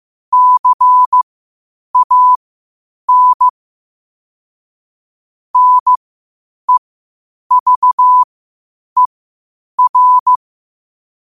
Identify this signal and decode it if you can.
Morse code, heard as a keyed tone, about 15 wpm, spaced as for 8 wpm, 1000 Hz. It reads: CAN NEVER
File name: qotdfiles / 8wpm